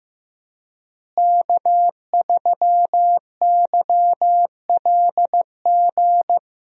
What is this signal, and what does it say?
Morse code, heard as a keyed tone, about 15 wpm, 700 Hz. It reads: K3YLG
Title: USA Call Signs - 5 Characters - Repeated 2x - 15wpm